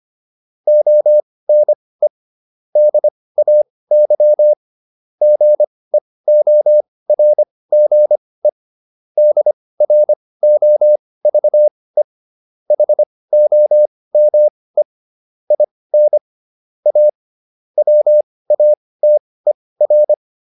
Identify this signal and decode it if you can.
Morse code, heard as a keyed tone, about 25 wpm, spaced as for 18 wpm, 600 Hz. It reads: ONE DAY GEORGE DROVE HOME IN A WATER